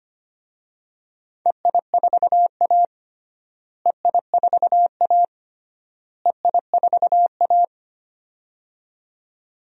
Morse code, heard as a keyed tone, about 25 wpm, 700 Hz. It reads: EI4A EI4A EI4A
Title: DXCC-100 Call Signs - 4 Characters - Repeated 3x - 25wpm